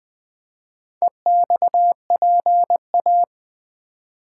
Morse code, heard as a keyed tone, about 20 wpm, 690 Hz. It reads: EXPA
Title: Build A Word - 2 Letters At A Time - 6-Letter Words - 20wpm